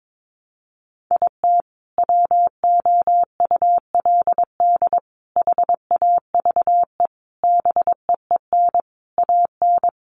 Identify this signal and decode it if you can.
Morse code, heard as a keyed tone, about 22 wpm, 700 Hz. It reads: IT WOULD HAVE BEEN AN
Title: Sentences from Top 1100 Words 22wpm